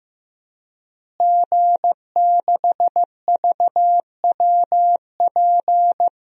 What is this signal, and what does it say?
Morse code, heard as a keyed tone, about 15 wpm, 700 Hz. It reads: G6VWP
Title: DXCC-100 Call Signs - 5 Characters - Repeated 2x - 15wpm